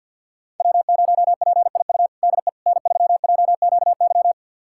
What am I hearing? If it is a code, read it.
Morse code, heard as a keyed tone, about 52 wpm, 700 Hz. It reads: W0PIU DE N3JQY
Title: USA Callsign DE Callsign - 5 Characters - 50wpm